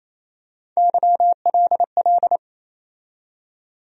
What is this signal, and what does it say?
Morse code, heard as a keyed tone, about 28 wpm, 700 Hz. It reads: YLL
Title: USA Call Sign Postfixes - 3 Letters 28wpm